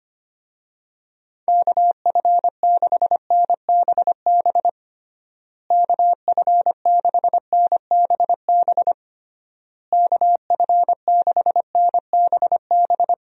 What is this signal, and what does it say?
Morse code, heard as a keyed tone, about 25 wpm, 700 Hz. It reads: KF6NBB KF6NBB KF6NBB
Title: USA Call Signs - 6 Characters - Repeated 3x - 25wpm